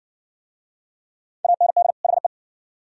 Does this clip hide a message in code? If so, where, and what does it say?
Yes: Morse code from 1.4 to 2.3 s, heard as a keyed tone, about 60 wpm, 700 Hz. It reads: AND HE